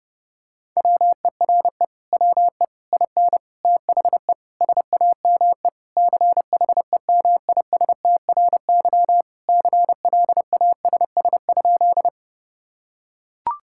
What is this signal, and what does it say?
Morse code, heard as a keyed tone, about 30 wpm, 700 Hz. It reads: WERE WE IN THE SAME CHEMISTRY CLASS?
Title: Sentences from Top 1000 Words - Review Part 6 - 30wpm